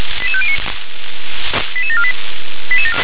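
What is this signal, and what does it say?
Five-tone selective call (ZVEI), heard as quick runs of five tones, tones of about 80 ms. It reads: 88509 88748
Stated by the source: ZVEI five-tone selective call received on an SDR.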